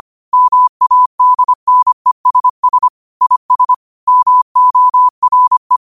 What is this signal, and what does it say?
Morse code, heard as a keyed tone, about 25 wpm, 1000 Hz. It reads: MADNESS IS MORE
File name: qotdfiles / 25wpm